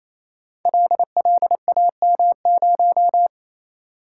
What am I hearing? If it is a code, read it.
Morse code, heard as a keyed tone, about 27 wpm, 700 Hz. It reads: LLAM0
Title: Code Groups - 5 Letter:Numbers - 28wpm